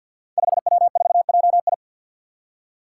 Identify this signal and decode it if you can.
Morse code, heard as a keyed tone, about 49 wpm, 700 Hz. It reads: 5W4JI
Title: Five Challenging Letter-Number Combinations 50wpm